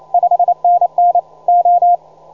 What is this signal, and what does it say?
Morse code, heard as a keyed tone, about 28 wpm, 720 Hz. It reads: 5NN O
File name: juma-trx2-80m-cw-sample